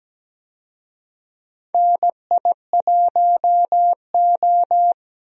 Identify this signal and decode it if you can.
Morse code, heard as a keyed tone, about 17 wpm, 700 Hz. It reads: NI1O
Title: USA Call Signs - 4 Characters - 17wpm